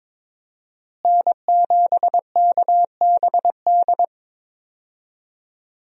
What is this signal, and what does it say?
Morse code, heard as a keyed tone, about 22 wpm, 700 Hz. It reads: N7KBD
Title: USA Call Signs - 5 Characters - Repeated 3x - 22wpm